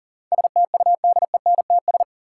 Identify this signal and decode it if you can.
Morse code, heard as a keyed tone, about 40 wpm, 700 Hz. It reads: STUDENTS
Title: Sets of 5 Words Encore - New Word Drill - 40wpm